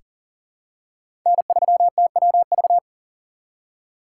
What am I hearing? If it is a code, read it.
Morse code, heard as a keyed tone, about 40 wpm, 700 Hz. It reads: N3TWV
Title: USA Call Signs - 2x Character Spacing - 40wpm